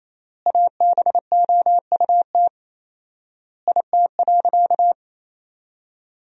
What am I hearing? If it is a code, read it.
Morse code, heard as a keyed tone, about 28 wpm, 700 Hz. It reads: ABOUT IT.
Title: 4x Word-Spacing - Sentences from Top 400 Words - 28wpm